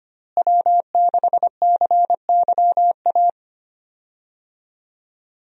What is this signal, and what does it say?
Morse code, heard as a keyed tone, about 25 wpm, 700 Hz. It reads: W6CYA
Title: USA Call Signs - 5 Characters - Repeated 2x - 25wpm